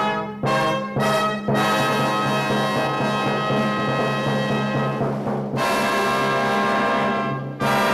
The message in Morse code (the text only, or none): none